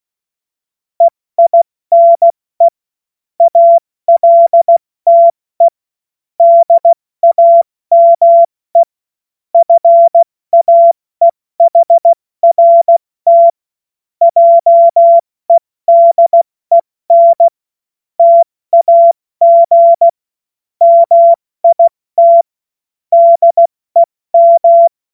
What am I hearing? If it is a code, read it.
Morse code, heard as a keyed tone, about 16 wpm, spaced as for 14 wpm, 680 Hz. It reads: EINE ALTE DAME FAEHRT JEDEN TAG MIT DEM
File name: Morsekurs / Witze / Witz 05. Oktober 2021, 70 F80